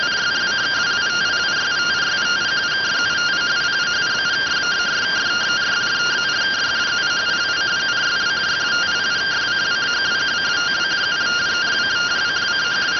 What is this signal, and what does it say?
Two-tone teleprinter signal, about 170 Hz shift.